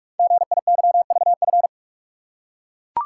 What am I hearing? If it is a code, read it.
Morse code, heard as a keyed tone, about 44 wpm, 700 Hz. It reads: GIYVF
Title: Code Groups - 5 Letters - 45wpm